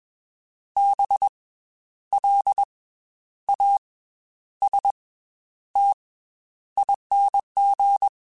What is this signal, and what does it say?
Morse code, heard as a keyed tone, about 21 wpm, 800 Hz. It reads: B L A S T ING
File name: Special INGs / Special INGs-mp3s / Slow ing @20